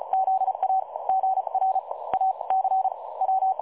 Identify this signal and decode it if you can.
Morse code, heard as a keyed tone, about 36 wpm, spaced as for 26 wpm, 760 Hz. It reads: ZA GRAC W
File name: dlazaawansowanych / nagraniazpasma / websdr recording 3565.0kHz